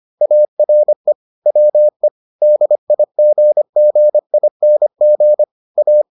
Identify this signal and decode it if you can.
Morse code, heard as a keyed tone, about 25 wpm, 600 Hz. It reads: ARE WE DIGGING A